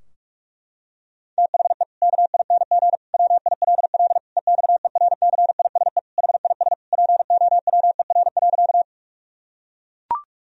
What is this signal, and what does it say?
Morse code, heard as a keyed tone, about 41 wpm, 700 Hz. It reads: THE KING WILL EXERCISE HIS POWER.
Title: Sentences from Top 1000 Words - Review Part 4 - 45wpm